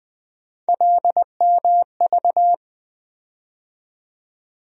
Morse code, heard as a keyed tone, about 20 wpm, 700 Hz. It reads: LMV
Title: USA Call Sign Postfixes - 3 Letters 20wpm